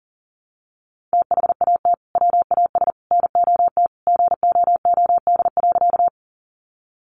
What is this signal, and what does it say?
Morse code, heard as a keyed tone, about 40 wpm, 700 Hz. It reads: THAT WAS NOT GOOD.
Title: Sentences from Top 600 Words 40wpm